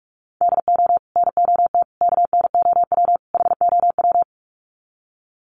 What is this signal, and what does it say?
Morse code, heard as a keyed tone, about 44 wpm, 700 Hz. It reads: DO NOT KNOW HOW
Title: Sets of 4 Words 45wpm